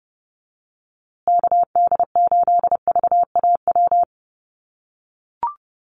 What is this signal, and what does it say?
Morse code, heard as a keyed tone, about 30 wpm, 700 Hz. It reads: KD8VAW
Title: USA Call Signs Repeated 3x 30wpm